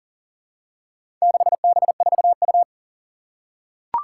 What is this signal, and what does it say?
Morse code, heard as a keyed tone, about 44 wpm, 690 Hz. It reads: 6B4U
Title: Four Challenging Letter-Number Combinations 40wpm